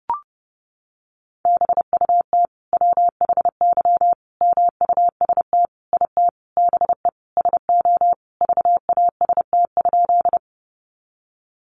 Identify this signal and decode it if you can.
Morse code, heard as a keyed tone, about 30 wpm, 700 Hz. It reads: BUT WHY MUST IT BE SO VAST?
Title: Sentences from Top 1300 Words 30wpm